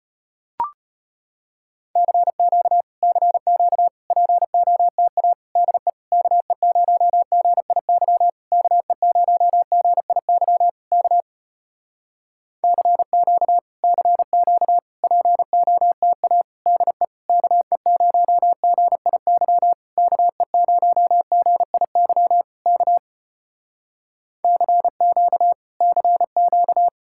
Morse code, heard as a keyed tone, about 33 wpm, 700 Hz. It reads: CQ CQ POTA DE KE0GIY KE0GIY K CQ CQ POTA DE KE0GIY KE0GIY K CQ CQ